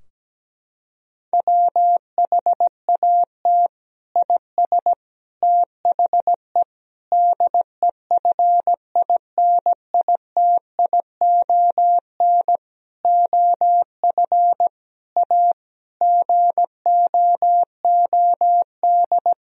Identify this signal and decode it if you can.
Morse code, heard as a keyed tone, about 17 wpm, 700 Hz. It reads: WHAT IS THE DEFINITION OF A GOOD
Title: Sentences from Top 1000 Words - Review Part 4 - 17wpm